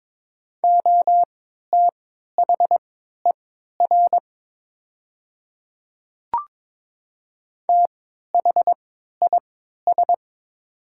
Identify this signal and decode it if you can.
Morse code, heard as a keyed tone, about 22 wpm, spaced as for 12 wpm, 700 Hz. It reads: OTHER THIS
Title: 3x Character Spacing - Top 100 Words - 22wpm